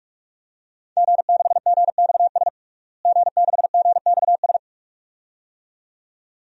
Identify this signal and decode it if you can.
Morse code, heard as a keyed tone, about 42 wpm, 700 Hz. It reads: G6GXS G6GXS
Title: DXCC-100 Call Signs - 5 Characters - Repeated 2x - 45wpm